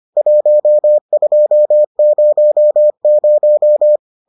Morse code, heard as a keyed tone, about 25 wpm, 600 Hz. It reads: 1200